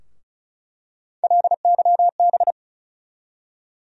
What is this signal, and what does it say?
Morse code, heard as a keyed tone, about 36 wpm, 700 Hz. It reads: LYB
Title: USA Call Sign Postfixes - 3 Letters 35wpm